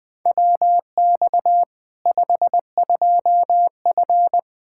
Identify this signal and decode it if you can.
Morse code, heard as a keyed tone, about 20 wpm, 700 Hz. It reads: WX 52F
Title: K1USN SST QSO Practice 20wpm